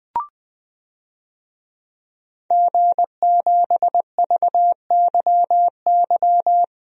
Morse code, heard as a keyed tone, about 21 wpm, 700 Hz. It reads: G7VYY
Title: DXCC-100 Call Signs - 5 Characters - Repeated 2x - 20wpm